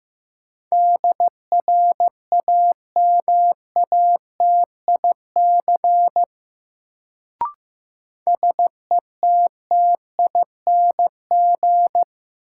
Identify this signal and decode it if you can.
Morse code, heard as a keyed tone, about 16 wpm, 700 Hz. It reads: DRAMATIC SETTING
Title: Sentences from Top 1300 Words - New Word Drill - 15wpm